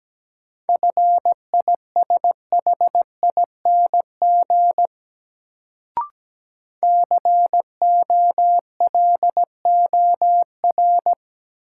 Morse code, heard as a keyed tone, about 17 wpm, 700 Hz. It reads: FISHING COLOR
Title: Sentences from Top 1600 Words - New Word Drill Alt - 17wpm